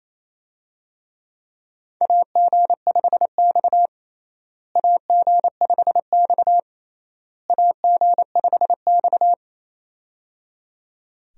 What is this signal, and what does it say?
Morse code, heard as a keyed tone, about 28 wpm, 700 Hz. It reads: AG5X AG5X AG5X
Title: USA Call Signs - 4 Characters - Repeated 3x - 28wpm